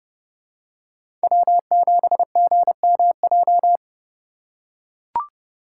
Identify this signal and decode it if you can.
Morse code, heard as a keyed tone, about 30 wpm, 700 Hz. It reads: W7GMJ
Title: USA Call Signs - 5 Characters - 4x Character Spacing - 30wpm